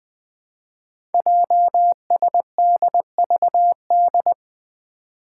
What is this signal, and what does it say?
Morse code, heard as a keyed tone, about 20 wpm, 700 Hz. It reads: JSDVD
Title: Code Groups - 5 Letters - 20wpm